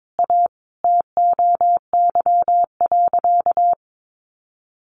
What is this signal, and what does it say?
Morse code, heard as a keyed tone, about 22 wpm, 700 Hz. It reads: A TOY.
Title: Easy Phrases 22wpm